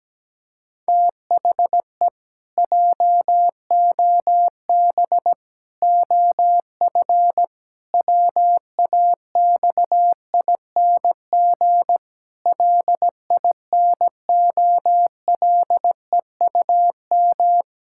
Morse code, heard as a keyed tone, about 17 wpm, 700 Hz. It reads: THE JOB OF WAXING LINOLEUM
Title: Pangram Challenge v2 - 17wpm